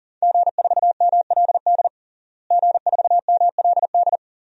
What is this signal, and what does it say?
Morse code, heard as a keyed tone, about 40 wpm, 700 Hz. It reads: G4MLD G4MLD
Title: DXCC-100 Call Signs - 5 Characters - Repeated 3x - 40wpm